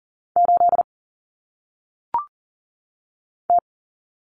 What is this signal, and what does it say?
Morse code, heard as a keyed tone, about 40 wpm, 700 Hz. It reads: 8 T
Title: Common Cut Numbers - 1-Digit 40wpm